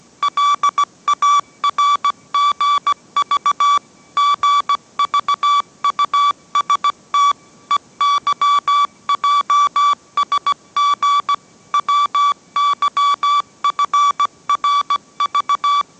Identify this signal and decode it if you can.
Morse code, heard as a keyed tone, about 17 wpm, 1150 Hz. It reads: LARGV GVUST EYJSG WYFRV